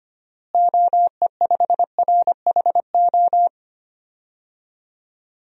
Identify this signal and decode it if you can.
Morse code, heard as a keyed tone, about 25 wpm, 700 Hz. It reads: OE5RHO
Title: DXCC-100 Call Signs - 6 Characters - Repeated 3x - 25wpm